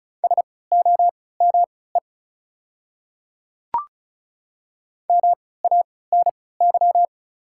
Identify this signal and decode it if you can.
Morse code, heard as a keyed tone, about 34 wpm, spaced as for 20 wpm, 700 Hz. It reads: SOME MANY
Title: 3x Character Spacing - Top 200 Words - 35wpm